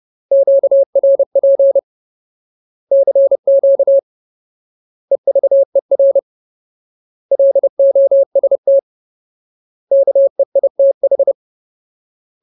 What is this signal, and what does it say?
Morse code, heard as a keyed tone, about 30 wpm, 550 Hz. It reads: QRP CQ EVER LOST KEITH
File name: Ultimate Word List / 30WPM Ultimate